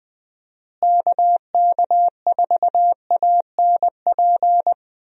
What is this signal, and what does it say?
Morse code, heard as a keyed tone, about 20 wpm, 700 Hz. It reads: KK4ANP
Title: USA Call Signs - 6 Characters - Repeated 2x - 20wpm